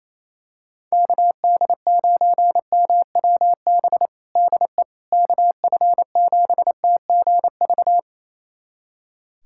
Morse code, heard as a keyed tone, about 28 wpm, 700 Hz. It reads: KD9MWB DE KF7TGV
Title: USA Callsign DE Callsign - 6 Characters - 28wpm